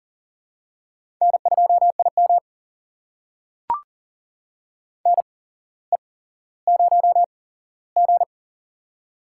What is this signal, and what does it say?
Morse code, heard as a keyed tone, about 40 wpm, spaced as for 17 wpm, 700 Hz. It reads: N2IM NE0G
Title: USA Call Signs - 4 Characters - 8x Character Spacing - 40wpm